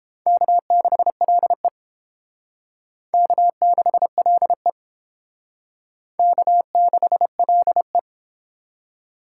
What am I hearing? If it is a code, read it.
Morse code, heard as a keyed tone, about 31 wpm, 700 Hz. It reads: K6LE K6LE K6LE